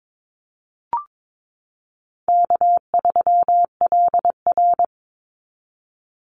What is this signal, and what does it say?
Morse code, heard as a keyed tone, about 22 wpm, 700 Hz. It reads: K3LR
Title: Easy Call Signs Repeated 3x - 22wpm